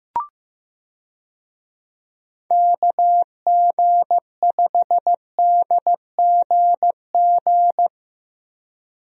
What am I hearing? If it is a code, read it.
Morse code, heard as a keyed tone, about 15 wpm, 700 Hz. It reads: KG5DGG
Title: USA Call Signs - 6 Characters - Repeated 3x - 15wpm